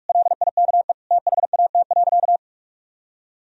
Morse code, heard as a keyed tone, about 46 wpm, 700 Hz. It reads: LIKE THAT.
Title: Sentences from Top 1000 Words - Review Part 3 - 45wpm